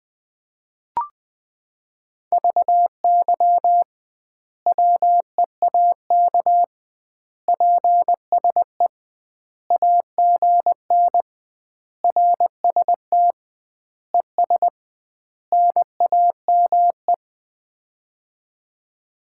Morse code, heard as a keyed tone, about 20 wpm, 700 Hz. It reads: VY WEAK PSE AGN RST ES NAME
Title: Common QSO Phrases - 2x Word-Spacing - 20wpm